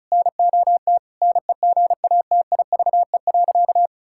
Morse code, heard as a keyed tone, about 36 wpm, 700 Hz. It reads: NOT NEGATIVE.
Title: Sentences from Top 1000 Words - Review Part 6 - 35wpm